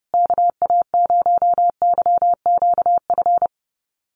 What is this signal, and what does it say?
Morse code, heard as a keyed tone, about 29 wpm, 700 Hz. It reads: KA0YQF
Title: USA Call Signs Repeated 2x 30wpm